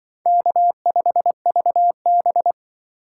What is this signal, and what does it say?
Morse code, heard as a keyed tone, about 24 wpm, 700 Hz. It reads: K5VB
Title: Speed Racing - USA 1x2 Call Signs 20wpm